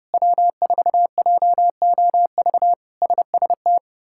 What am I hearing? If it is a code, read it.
Morse code, heard as a keyed tone, about 30 wpm, 700 Hz. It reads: W4JOV SST
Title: K1USN SST QSO Practice - 30wpm